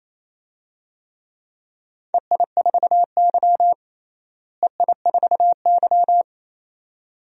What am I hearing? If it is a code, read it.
Morse code, heard as a keyed tone, about 27 wpm, 700 Hz. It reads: EI4Y EI4Y
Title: DXCC-100 Call Signs - 4 Characters - Repeated 2x - 28wpm